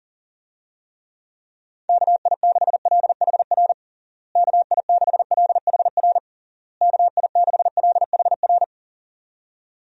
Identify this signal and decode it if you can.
Morse code, heard as a keyed tone, about 40 wpm, 700 Hz. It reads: KI6LHR KI6LHR KI6LHR
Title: USA Call Signs - 6 Characters - Repeated 3x - 40wpm